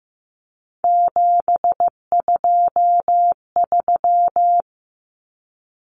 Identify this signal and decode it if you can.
Morse code, heard as a keyed tone, about 15 wpm, 700 Hz. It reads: 723
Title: Numbers - 3-Digit 15wpm